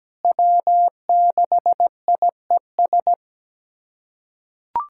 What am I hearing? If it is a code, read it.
Morse code, heard as a keyed tone, about 17 wpm, 700 Hz. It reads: W6IES
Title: USA Callsign DE Callsign - 5 Characters - 17wpm